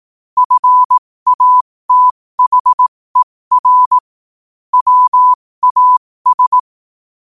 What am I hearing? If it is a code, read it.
Morse code, heard as a keyed tone, about 18 wpm, spaced as for 15 wpm, 1000 Hz. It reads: FATHER WAS
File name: qotdfiles / 15wpm